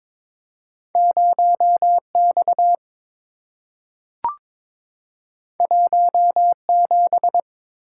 Morse code, heard as a keyed tone, about 22 wpm, 700 Hz. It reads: 0X 17
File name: practice / 22wpm / Two Letter-Number Combinations - 22wpm